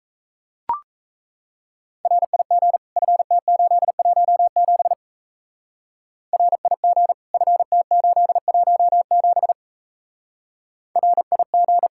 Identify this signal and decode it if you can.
Morse code, heard as a keyed tone, about 33 wpm, 700 Hz. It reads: RIG FT817 RIG FT817 RIG
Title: Speed-Racing Common QSO Phrases - 28wpm